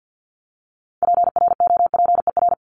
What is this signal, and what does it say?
Morse code, heard as a keyed tone, about 50 wpm, 700 Hz. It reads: PROPER
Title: LY Suffix - Top 100 Words with Base Words 50wpm